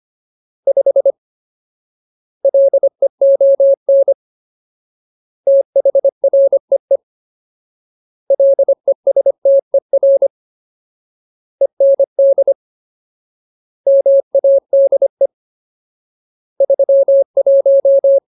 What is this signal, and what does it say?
Morse code, heard as a keyed tone, about 25 wpm, 550 Hz. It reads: 5 LEON THREE LESTER END MADE 31
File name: Ultimate Word List / 25WPM Ultimate